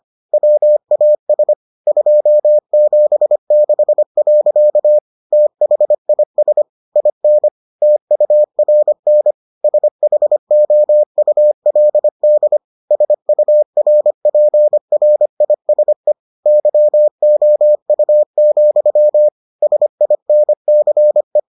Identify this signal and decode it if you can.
Morse code, heard as a keyed tone, about 25 wpm, 600 Hz. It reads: WAS 276. THIS IN TURN SHOULD SURPRISE YOU, SINCE